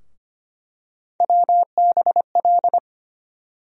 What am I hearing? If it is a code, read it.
Morse code, heard as a keyed tone, about 25 wpm, 700 Hz. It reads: WBL
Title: USA Call Sign Postfixes - 3 Letters 25wpm